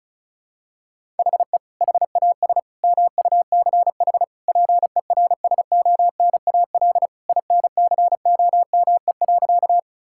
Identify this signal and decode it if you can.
Morse code, heard as a keyed tone, about 34 wpm, 700 Hz. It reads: HE HAS MUCH PERSONAL INCOME.
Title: Sentences from Top 1000 Words - Review Part 5 - 35wpm